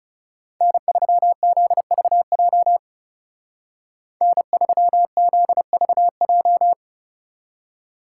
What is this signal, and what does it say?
Morse code, heard as a keyed tone, about 37 wpm, 700 Hz. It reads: N3ZVJ N3ZVJ
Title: Speed Racing - USA 1x3 Call Signs 30wpm